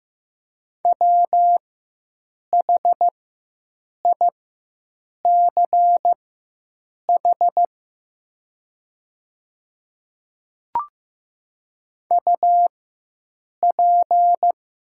Morse code, heard as a keyed tone, about 15 wpm, spaced as for 7 wpm, 700 Hz. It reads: WHICH UP
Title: 4x Character Spacing - Top 200 Words - 15wpm